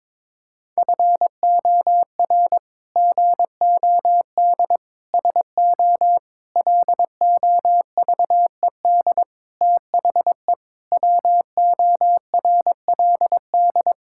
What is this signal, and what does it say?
Morse code, heard as a keyed tone, about 22 wpm, 700 Hz. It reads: FOR GOD SO LOVED THE WORLD
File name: practice / 22wpm / Sets of 8 Words - 22wpm